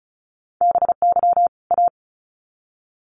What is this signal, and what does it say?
Morse code, heard as a keyed tone, about 34 wpm, 700 Hz. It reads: BY A
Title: Sets of 2 Words 35wpm